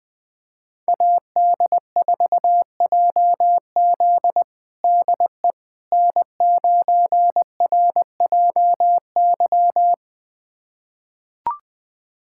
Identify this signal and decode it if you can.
Morse code, heard as a keyed tone, about 20 wpm, 700 Hz. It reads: AD4JZ DE N9RJY
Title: USA Callsign DE Callsign - 5 Characters - 20wpm